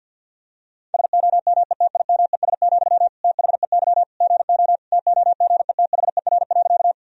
Morse code, heard as a keyed tone, about 50 wpm, 700 Hz. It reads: SOMETIMES, THEY GO TOGETHER.
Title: Sentences from Top 1000 Words - Review Part 3 - 50wpm